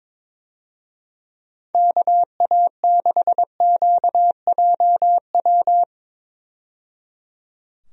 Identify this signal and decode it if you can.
Morse code, heard as a keyed tone, about 22 wpm, 700 Hz. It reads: KA6QJW
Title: USA Call Signs - 6 Characters - 22wpm